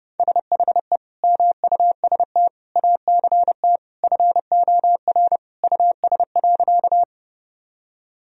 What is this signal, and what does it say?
Morse code, heard as a keyed tone, about 30 wpm, 700 Hz. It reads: SHE MUST ACT FOR US.
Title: Sentences from Top 1000 Words - Review Part 1 - 30wpm